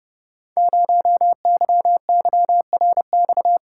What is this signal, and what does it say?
Morse code, heard as a keyed tone, about 29 wpm, 700 Hz. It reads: 0YYRX
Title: Code Groups - 5 Letter:Numbers - 30wpm